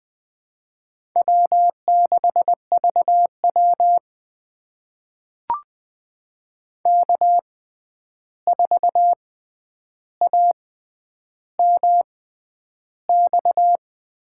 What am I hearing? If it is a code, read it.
Morse code, heard as a keyed tone, about 20 wpm, spaced as for 10 wpm, 700 Hz. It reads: W6VW K4AMX
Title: USA Call Signs - 6x Character Spacing - 20wpm